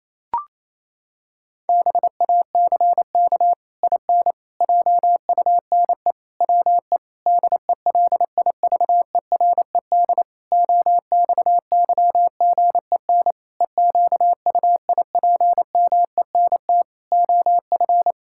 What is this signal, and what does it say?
Morse code, heard as a keyed tone, about 28 wpm, 700 Hz. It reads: BACK IN JUNE WE DELIVERED OXYGEN EQUIPMENT OF
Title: Pangram Challenge v2 - 28wpm